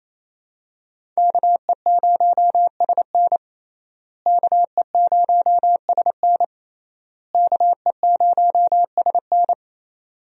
Morse code, heard as a keyed tone, about 28 wpm, 700 Hz. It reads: KE0SN KE0SN KE0SN
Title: USA Call Signs - 5 Characters - Repeated 3x - 28wpm